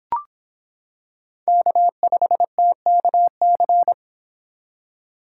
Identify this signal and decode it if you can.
Morse code, heard as a keyed tone, about 26 wpm, 700 Hz. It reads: K5TKC